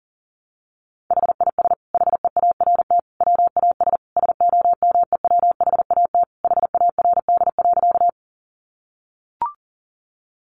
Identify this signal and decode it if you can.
Morse code, heard as a keyed tone, about 40 wpm, 700 Hz. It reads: HIS HEART WAS SOMEWHAT HARD.
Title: Sentences from Top 1300 Words 40wpm